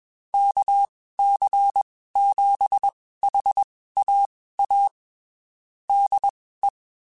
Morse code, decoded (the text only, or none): KC7HAA DE